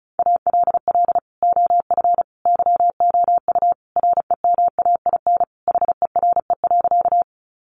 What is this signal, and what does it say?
Morse code, heard as a keyed tone, about 34 wpm, 700 Hz. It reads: ALL OF YOU REMAIN HERE.
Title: Sentences from Top 1200 Words 35wpm